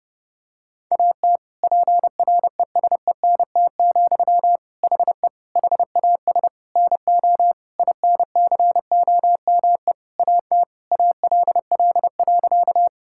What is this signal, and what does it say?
Morse code, heard as a keyed tone, about 30 wpm, 700 Hz. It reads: AT PRESENT, HE HAS NO INCOME AT ALL.
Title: Sentences from Top 1000 Words - Review Part 5 - 30wpm